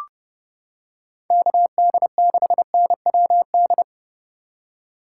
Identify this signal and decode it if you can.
Morse code, heard as a keyed tone, about 29 wpm, 700 Hz. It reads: KD6NWD